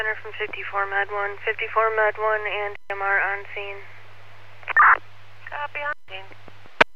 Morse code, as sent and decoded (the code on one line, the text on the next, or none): none